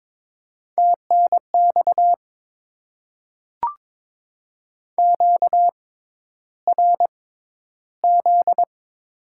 Morse code, heard as a keyed tone, about 22 wpm, spaced as for 11 wpm, 700 Hz. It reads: TNX QRZ
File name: Top 90 QSO Elements - 6x Character Spacing - 22wpm